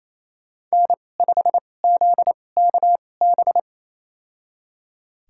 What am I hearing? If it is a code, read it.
Morse code, heard as a keyed tone, about 28 wpm, spaced as for 20 wpm, 700 Hz. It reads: N5ZKB
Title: USA Call Signs - 2x Character Spacing - 28wpm